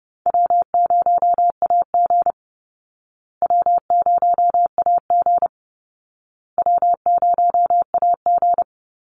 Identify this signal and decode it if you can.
Morse code, heard as a keyed tone, about 30 wpm, 700 Hz. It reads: W0AG W0AG W0AG
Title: Call Signs Repeated 2x - 30wpm